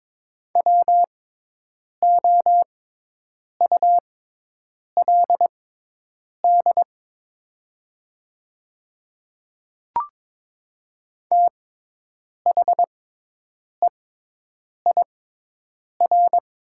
Morse code, heard as a keyed tone, about 22 wpm, spaced as for 8 wpm, 700 Hz. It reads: WOULD THEIR
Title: 6x Character Spacing - Top 100 Words - 22wpm